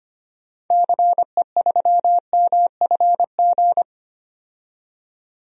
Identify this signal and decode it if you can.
Morse code, heard as a keyed tone, about 25 wpm, 700 Hz. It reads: CE3MFG
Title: DXCC-100 Call Signs - 6 Characters - 25wpm